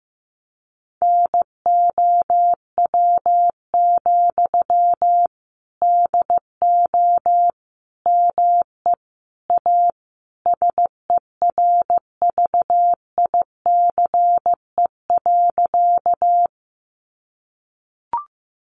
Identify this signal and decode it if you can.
Morse code, heard as a keyed tone, about 14 wpm, 700 Hz. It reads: NOW, DO ME A SERVICE.
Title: Sentences from Top 700 Words 15wpm